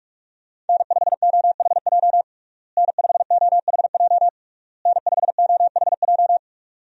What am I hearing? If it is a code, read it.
Morse code, heard as a keyed tone, about 46 wpm, 700 Hz. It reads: N5OHJ N5OHJ N5OHJ
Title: USA Call Signs - 5 Characters - Repeated 3x - 45wpm